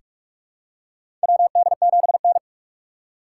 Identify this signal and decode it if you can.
Morse code, heard as a keyed tone, about 46 wpm, 700 Hz. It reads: WD7N
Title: USA Call Signs - 4 Characters - 6x Character Spacing - 45wpm